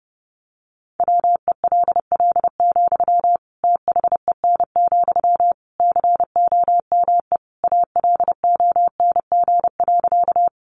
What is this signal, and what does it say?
Morse code, heard as a keyed tone, about 30 wpm, 700 Hz. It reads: WELL, THEN, COME ALONG.
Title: Sentences from Top 700 Words 30wpm